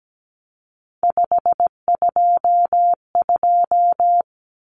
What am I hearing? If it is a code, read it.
Morse code, heard as a keyed tone, about 17 wpm, 700 Hz. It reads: 522
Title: Numbers - 3-Digit 17wpm